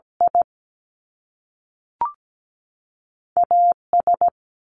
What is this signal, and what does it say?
Morse code, heard as a keyed tone, about 17 wpm, 700 Hz. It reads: I A S
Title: Top Two Letter Combinations in Words 17wpm